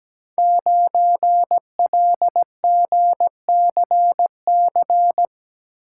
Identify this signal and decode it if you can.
Morse code, heard as a keyed tone, about 17 wpm, 700 Hz. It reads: 9LGCC
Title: Code Groups - 5 Letter:Numbers - 17wpm